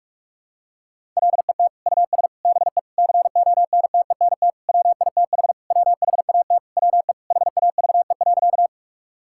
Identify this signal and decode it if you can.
Morse code, heard as a keyed tone, about 41 wpm, 700 Hz. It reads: LET US BE CONTENT WITH WHAT WE HAVE.
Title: Sentences from Top 1000 Words - Review Part 6 - 45wpm